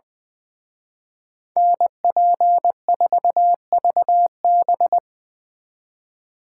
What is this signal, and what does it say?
Morse code, heard as a keyed tone, about 24 wpm, 700 Hz. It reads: NP4VB
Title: Speed Racing - USA 2x2 Call Signs 20wpm